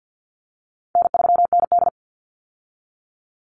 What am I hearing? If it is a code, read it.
Morse code, heard as a keyed tone, about 50 wpm, 700 Hz. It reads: N3ND
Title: Easy Call Signs - 50wpm